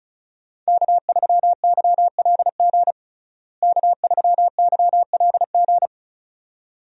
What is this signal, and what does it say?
Morse code, heard as a keyed tone, about 36 wpm, 700 Hz. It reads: K3YLG K3YLG
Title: USA Call Signs - 5 Characters - Repeated 2x - 35wpm